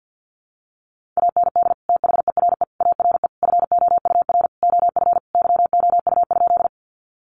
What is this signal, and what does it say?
Morse code, heard as a keyed tone, about 50 wpm, 700 Hz. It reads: AND THERE ARE FOUR OF YOU?
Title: Sentences from Top 300 Words 50wpm